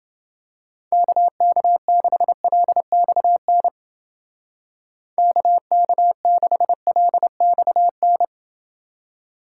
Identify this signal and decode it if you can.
Morse code, heard as a keyed tone, about 29 wpm, 700 Hz. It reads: KK6LXN KK6LXN